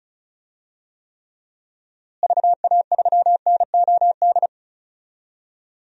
Morse code, heard as a keyed tone, about 34 wpm, 700 Hz. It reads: VA3NOD
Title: DXCC-100 Call Signs - 6 Characters - Repeated 2x - 35wpm